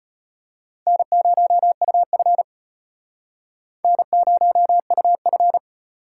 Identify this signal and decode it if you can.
Morse code, heard as a keyed tone, about 31 wpm, 700 Hz. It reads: N0UF N0UF